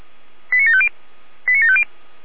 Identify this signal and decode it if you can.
Five-tone selective call (ZVEI), heard as quick runs of five tones, tones of about 70 ms. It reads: 89750 89750
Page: /455/SDR/ZVEI/89456/2022.02.23/